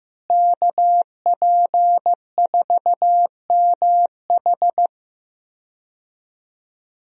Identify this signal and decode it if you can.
Morse code, heard as a keyed tone, about 15 wpm, 700 Hz. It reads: KP4MH
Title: DXCC-100 Call Signs - 5 Characters - Repeated 2x - 15wpm